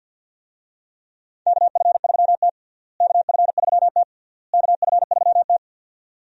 Morse code, heard as a keyed tone, about 50 wpm, 700 Hz. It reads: KF3T KF3T KF3T
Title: USA Call Signs - 4 Characters - Repeated 3x - 50wpm